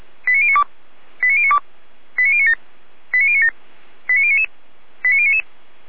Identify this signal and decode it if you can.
Five-tone selective call (ZVEI), heard as quick runs of five tones, tones of about 70 ms. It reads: 89092 89092 89097 89097 89099 89099